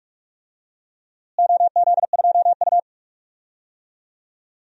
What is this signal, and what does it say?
Morse code, heard as a keyed tone, about 45 wpm, 700 Hz. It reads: OZ2U
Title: DXCC-100 Call Signs - 4 Characters - Repeated 2x - 45wpm